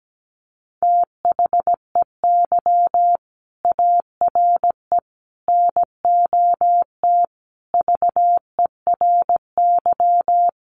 Morse code, heard as a keyed tone, about 17 wpm, 700 Hz. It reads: THEY ARE NOT VERY
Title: Sentences from Top 1400 Words 17wpm